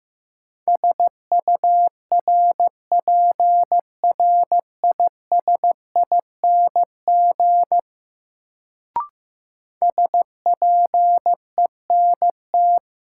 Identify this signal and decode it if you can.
Morse code, heard as a keyed tone, about 15 wpm, 700 Hz. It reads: SURPRISING SPENT
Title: Sets of 5 Words - New Word Drill - 15wpm